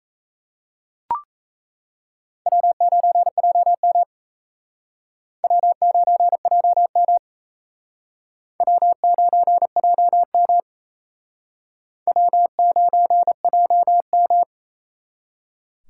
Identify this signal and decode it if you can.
Morse code, heard as a keyed tone, about 35 wpm, 700 Hz. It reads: W9JM W9JM W9JM W9JM
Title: Speed Racing - USA 1x2 Call Signs 28wpm